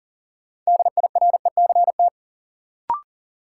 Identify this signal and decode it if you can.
Morse code, heard as a keyed tone, about 40 wpm, 700 Hz. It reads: DIRECT
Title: Sentences from Top 400 Words - New Word Drill - 40wpm